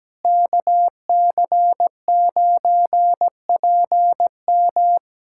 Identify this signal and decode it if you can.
Morse code, heard as a keyed tone, about 17 wpm, 700 Hz. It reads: KC9PM
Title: USA Callsign DE Callsign - 5 Characters - 17wpm